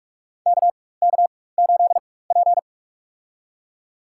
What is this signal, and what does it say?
Morse code, heard as a keyed tone, about 46 wpm, spaced as for 21 wpm, 700 Hz. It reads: KK8P
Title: USA Call Signs - 4 Characters - 4x Character Spacing - 45wpm